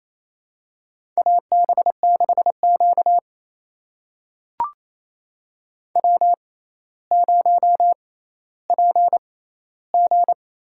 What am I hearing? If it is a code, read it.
Morse code, heard as a keyed tone, about 28 wpm, spaced as for 14 wpm, 700 Hz. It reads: AB6Q W0PG
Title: USA Call Signs - 4 Characters - 6x Character Spacing - 28wpm